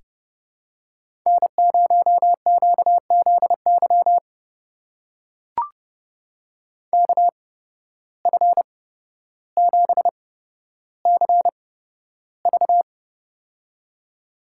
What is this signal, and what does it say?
Morse code, heard as a keyed tone, about 29 wpm, spaced as for 13 wpm, 700 Hz. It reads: N0QZY KF7CV
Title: USA Call Signs - 5 Characters - 8x Character Spacing - 30wpm